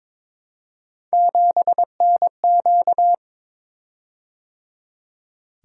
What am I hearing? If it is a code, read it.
Morse code, heard as a keyed tone, about 22 wpm, 700 Hz. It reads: 7NQ
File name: practice / 22wpm / Three Letter-Number Combinations - 22wpm